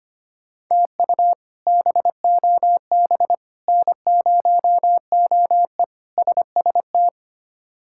Morse code, heard as a keyed tone, about 25 wpm, 700 Hz. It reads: TU BOB N0OE SST
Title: K1USN SST QSO Practice - 25wpm